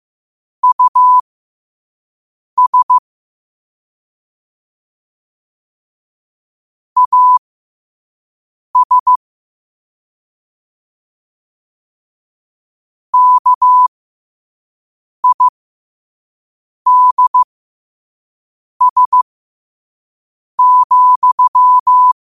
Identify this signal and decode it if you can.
Morse code, heard as a keyed tone, about 15 wpm, spaced as for 5 wpm, 1000 Hz. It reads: US AS KIDS,
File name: qotdfiles / 5wpm